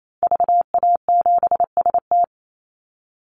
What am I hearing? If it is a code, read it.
Morse code, heard as a keyed tone, about 28 wpm, 700 Hz. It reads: VA7ST
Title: Call Signs Repeated 3x - 28wpm